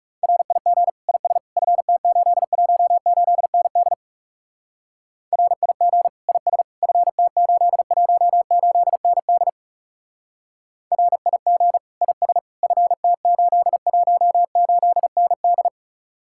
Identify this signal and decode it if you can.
Morse code, heard as a keyed tone, about 37 wpm, 700 Hz. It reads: RIG IS FT818ND RIG IS FT818ND RIG IS FT818ND
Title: Speed-Racing Common QSO Phrases Encore - 30wpm